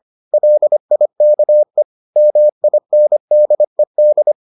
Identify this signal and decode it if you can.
Morse code, heard as a keyed tone, about 25 wpm, 600 Hz. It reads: LIKE MINDED